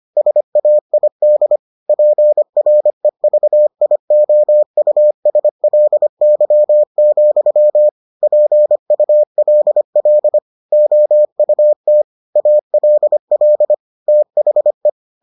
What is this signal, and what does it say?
Morse code, heard as a keyed tone, about 25 wpm, 600 Hz. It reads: SAID PREVIOUSLY, PULL OUT ALL THE